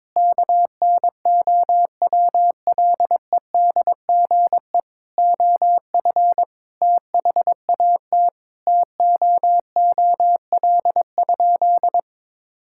Morse code, heard as a keyed tone, about 22 wpm, 700 Hz. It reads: KNOWLEDGE OF THAT TOOL?
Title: Sentences from Top 1000 Words - Review Part 6 - 22wpm